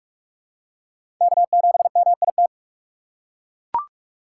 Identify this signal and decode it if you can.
Morse code, heard as a keyed tone, about 42 wpm, 700 Hz. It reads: K7MIT
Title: USA Call Signs - 5 Characters - 3x Character Spacing - 45wpm